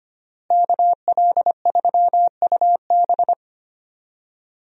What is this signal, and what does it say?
Morse code, heard as a keyed tone, about 25 wpm, 700 Hz. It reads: KL3UB
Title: USA Call Signs - 5 Characters - Repeated 2x - 25wpm